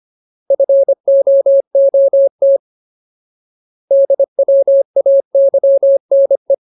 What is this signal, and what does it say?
Morse code, heard as a keyed tone, about 25 wpm, 550 Hz. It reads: FOOT DWAYNE
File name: Ultimate Word List / 25WPM Ultimate